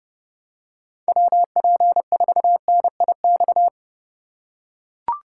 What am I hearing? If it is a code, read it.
Morse code, heard as a keyed tone, about 31 wpm, 700 Hz. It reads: WP4NIX